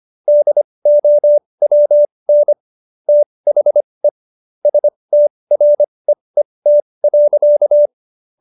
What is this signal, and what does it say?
Morse code, heard as a keyed tone, about 25 wpm, spaced as for 20 wpm, 590 Hz. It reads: DOWN THE STREET.